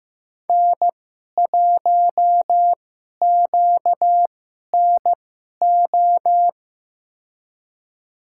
Morse code, heard as a keyed tone, about 15 wpm, spaced as for 11 wpm, 700 Hz. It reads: N1QNO
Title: USA Call Signs - 2x Character Spacing - 15wpm